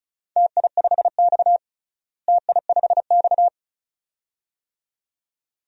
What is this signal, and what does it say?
Morse code, heard as a keyed tone, about 36 wpm, 700 Hz. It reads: TI5X TI5X
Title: DXCC-100 Call Signs - 4 Characters - Repeated 2x - 35wpm